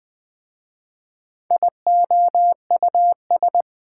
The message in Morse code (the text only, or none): IOUS